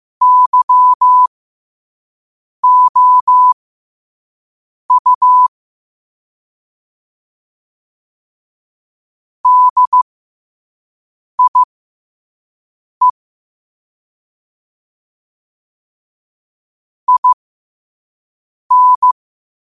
Morse code, heard as a keyed tone, about 15 wpm, spaced as for 5 wpm, 1000 Hz. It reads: YOU DIE IN
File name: qotdfiles / 5wpm